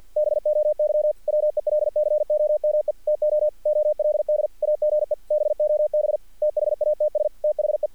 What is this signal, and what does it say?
Morse code, heard as a keyed tone, about 46 wpm, 600 Hz. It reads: BOY WELCOME TO OLD AGE BOB THATS THE
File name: archived / CRSnet